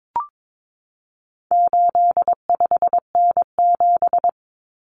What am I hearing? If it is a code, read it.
Morse code, heard as a keyed tone, about 22 wpm, 700 Hz. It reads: 85N7
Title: Common Cut Numbers - 4-Digit 22wpm